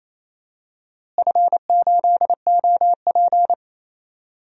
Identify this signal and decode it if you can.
Morse code, heard as a keyed tone, about 28 wpm, 700 Hz. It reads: F8OP
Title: DXCC-100 Call Signs - 4 Characters - Repeated 3x - 28wpm